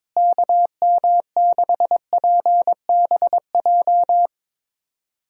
Morse code, heard as a keyed tone, about 22 wpm, 700 Hz. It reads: KM6PBJ
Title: USA Call Signs - 6 Characters - Repeated 2x - 22wpm